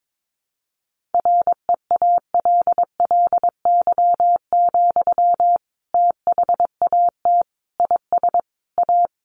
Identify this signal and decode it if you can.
Morse code, heard as a keyed tone, about 22 wpm, 700 Hz. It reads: REALLY, THAT IS A
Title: Sentences from Top 600 Words 22wpm